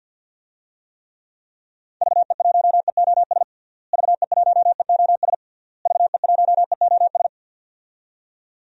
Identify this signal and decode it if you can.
Morse code, heard as a keyed tone, about 46 wpm, 700 Hz. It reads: VE1EOS VE1EOS VE1EOS
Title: DXCC-100 Call Signs - 6 Characters - Repeated 3x - 50wpm